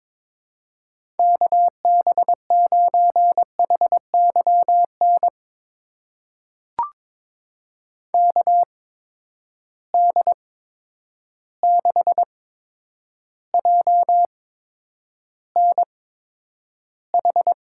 Morse code, heard as a keyed tone, about 22 wpm, spaced as for 9 wpm, 700 Hz. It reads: KB9HYN KD6JNH
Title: USA Call Signs - 6 Characters - 8x Character Spacing - 22wpm